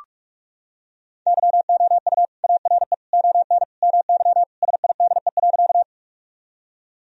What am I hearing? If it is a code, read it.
Morse code, heard as a keyed tone, about 42 wpm, 700 Hz. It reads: YOU ARE ON MY SIDE.
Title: Sentences from Top 1000 Words - Review Part 1 - 45wpm